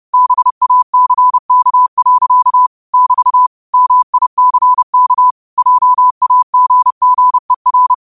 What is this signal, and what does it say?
Morse code, heard as a keyed tone, about 30 wpm, 1000 Hz. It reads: BACK. <BT> MICK JAGGER
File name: qotdfiles / 30wpm